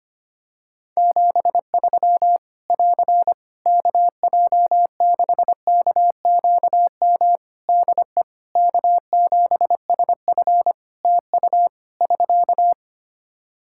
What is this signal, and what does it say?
Morse code, heard as a keyed tone, about 25 wpm, 700 Hz. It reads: 73 <AR> KJ6KQM DE K7SF TU <SK>
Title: Practice QSO - Exchange 4 - Ending - 25wpm